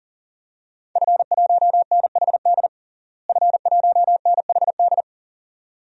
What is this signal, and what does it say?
Morse code, heard as a keyed tone, about 37 wpm, 700 Hz. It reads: F1NHD F1NHD
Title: DXCC-100 Call Signs - 5 Characters - Repeated 2x - 40wpm